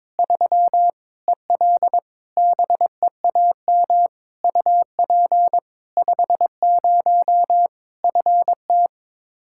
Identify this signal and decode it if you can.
Morse code, heard as a keyed tone, about 22 wpm, 700 Hz. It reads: 3 EL BEAM UP 50 FT
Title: Practice QSO - Exchange 2 - Info - 22wpm